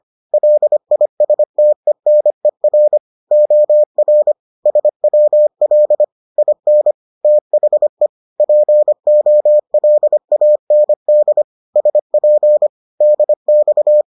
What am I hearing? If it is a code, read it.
Morse code, heard as a keyed tone, about 25 wpm, 600 Hz. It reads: LISTENER OR SWL IN THE POLAND SP DX